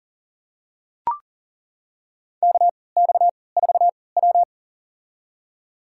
Morse code, heard as a keyed tone, about 40 wpm, spaced as for 23 wpm, 700 Hz. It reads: KX4W
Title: USA Call Signs - 4 Characters - 3x Character Spacing - 40wpm